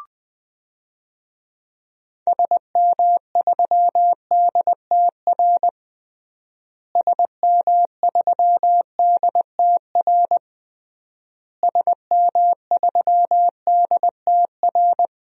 Morse code, heard as a keyed tone, about 21 wpm, 700 Hz. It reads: SM3DTR SM3DTR SM3DTR
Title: DXCC-100 Call Signs - 6 Characters - Repeated 3x - 20wpm